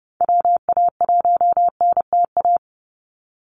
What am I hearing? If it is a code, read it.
Morse code, heard as a keyed tone, about 30 wpm, 700 Hz. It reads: WA1NTA
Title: Call Signs Repeated 3x - 30wpm